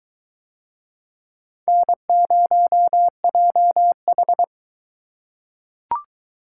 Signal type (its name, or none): Morse code